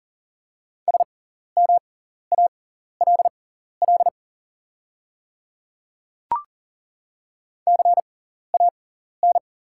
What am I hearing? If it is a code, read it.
Morse code, heard as a keyed tone, about 40 wpm, spaced as for 14 wpm, 700 Hz. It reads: SMALL CAN
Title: 6x Character Spacing - Top 200 Words - 40wpm